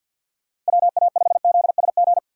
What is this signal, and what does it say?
Morse code, heard as a keyed tone, about 53 wpm, 700 Hz. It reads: WA5ZSG
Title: USA Call Signs - 6 Characters - 3x Character Spacing - 50wpm